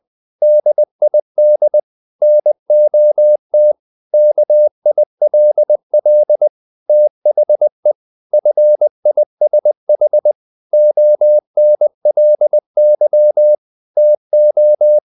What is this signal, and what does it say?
Morse code, heard as a keyed tone, about 20 wpm, 600 Hz. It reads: DID NOT KILL THE FISH ONLY TO
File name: the old man and the sea part 2 20wpm